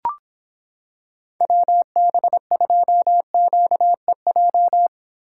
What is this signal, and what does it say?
Morse code, heard as a keyed tone, about 26 wpm, 700 Hz. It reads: WB2QEJ